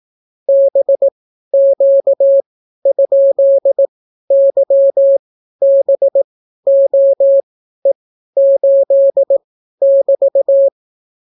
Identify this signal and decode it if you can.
Morse code, heard as a keyed tone, about 18 wpm, spaced as for 12 wpm, 550 Hz. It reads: BQ?YBOE8<BT>